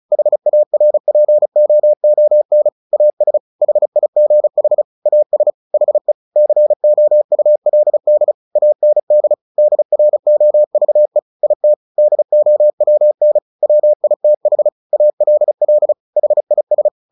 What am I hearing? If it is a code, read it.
Morse code, heard as a keyed tone, about 34 wpm, 600 Hz. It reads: HARPOON AS HIGH AS HE COULD AND DROVE IT DOWN WITH ALL HIS
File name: the old man and the sea 2 35wpm